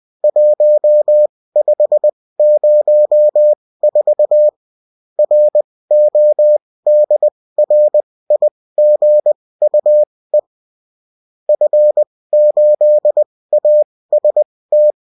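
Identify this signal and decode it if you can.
Morse code, heard as a keyed tone, about 20 wpm, spaced as for 16 wpm, 600 Hz. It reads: 1504 RODRIGUE F8AST
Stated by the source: Morse code bulletin of a French amateur CW union, sent as text and in CW.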